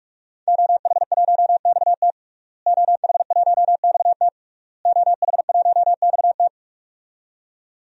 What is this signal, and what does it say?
Morse code, heard as a keyed tone, about 46 wpm, 700 Hz. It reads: OH1XT OH1XT OH1XT
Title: DXCC-100 Call Signs - 5 Characters - Repeated 3x - 45wpm